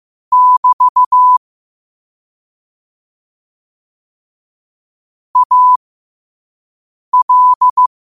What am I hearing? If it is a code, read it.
Morse code, heard as a keyed tone, about 15 wpm, spaced as for 5 wpm, 1000 Hz. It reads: <BT> AL